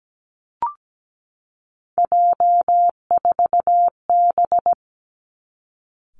Morse code, heard as a keyed tone, about 17 wpm, 700 Hz. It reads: J4B
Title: Three Challenging Letter-Number Combinations 17wpm